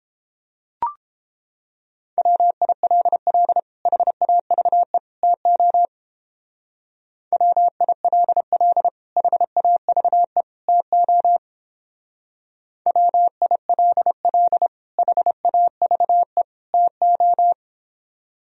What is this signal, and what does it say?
Morse code, heard as a keyed tone, about 31 wpm, 700 Hz. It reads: WILL HAVE TO WILL HAVE TO WILL HAVE TO